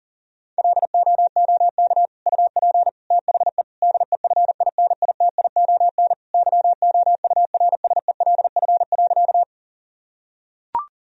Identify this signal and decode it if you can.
Morse code, heard as a keyed tone, about 40 wpm, 700 Hz. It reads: LOOK UP THE DEFINITION YOURSELF.
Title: Sentences from Top 1000 Words - Review Part 4 - 40wpm